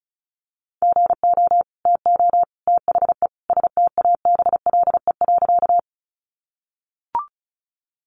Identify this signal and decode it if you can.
Morse code, heard as a keyed tone, about 34 wpm, 700 Hz. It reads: GO TO THE STABLE.
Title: Sentences from Top 1500 Words 35wpm